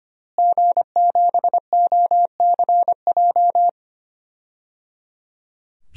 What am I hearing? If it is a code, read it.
Morse code, heard as a keyed tone, about 25 wpm, 700 Hz. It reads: G7OCJ
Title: DXCC-100 Call Signs - 5 Characters - Repeated 3x - 25wpm